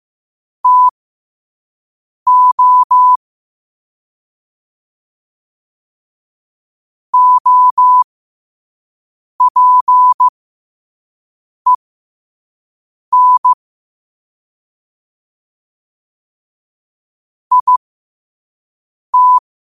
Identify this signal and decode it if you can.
Morse code, heard as a keyed tone, about 15 wpm, spaced as for 5 wpm, 1000 Hz. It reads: TO OPEN IT
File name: qotdfiles / 5wpm